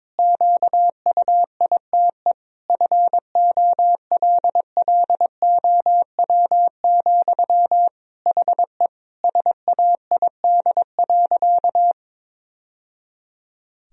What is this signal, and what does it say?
Morse code, heard as a keyed tone, about 22 wpm, 700 Hz. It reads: QUITE FOLLOW, HE SAID.
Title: Sentences from Top 1000 Words - Review Part 1 - 22wpm